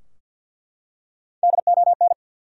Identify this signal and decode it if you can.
Morse code, heard as a keyed tone, about 47 wpm, 700 Hz. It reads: DON
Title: USA Common Names - Max 5 Letters - 50wpm